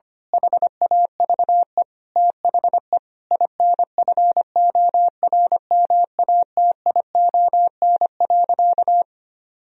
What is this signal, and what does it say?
Morse code, heard as a keyed tone, about 25 wpm, 700 Hz. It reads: HAVE THE INFORMATION.
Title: Sentences from Top 1000 Words - Review Part 3 - 25wpm